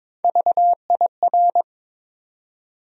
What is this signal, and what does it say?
Morse code, heard as a keyed tone, about 22 wpm, 700 Hz. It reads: VIR
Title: USA Call Sign Postfixes - 3 Letters 22wpm